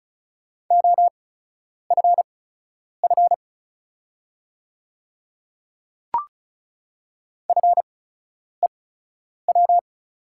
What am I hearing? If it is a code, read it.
Morse code, heard as a keyed tone, about 34 wpm, spaced as for 10 wpm, 700 Hz. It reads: OFF FEW
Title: 8x Character Spacing - Top 300 Words - 35wpm